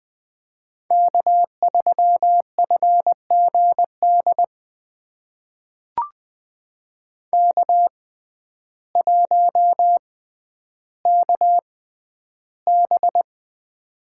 Morse code, heard as a keyed tone, about 20 wpm, spaced as for 11 wpm, 700 Hz. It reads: K3FGD K1KB
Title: USA Call Signs - 6x Character Spacing - 20wpm